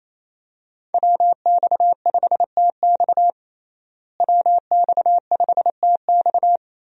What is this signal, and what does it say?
Morse code, heard as a keyed tone, about 27 wpm, 700 Hz. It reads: WX5TX WX5TX
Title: USA Call Signs - 5 Characters - Repeated 3x - 28wpm